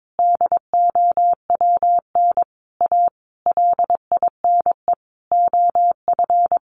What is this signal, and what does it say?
Morse code, heard as a keyed tone, about 22 wpm, 700 Hz. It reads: DOWN A LINE OF
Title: A Sound Of Thunder 22wpm